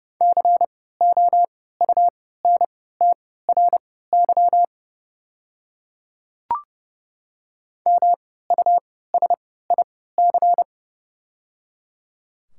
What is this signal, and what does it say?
Morse code, heard as a keyed tone, about 30 wpm, spaced as for 17 wpm, 700 Hz. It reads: COUNTRY MUSIC
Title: 3x Character Spacing - Top 500 Words - 30wpm